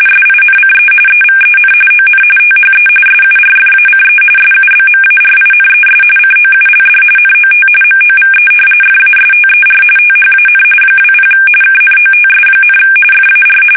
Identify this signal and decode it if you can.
RTTY, about 45.45 baud. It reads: SINCE EVEN ON THE SURFACE SUBMARINES COULD NOT MOVE FASTER THAN 20 KNOTS, FAR
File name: fsk850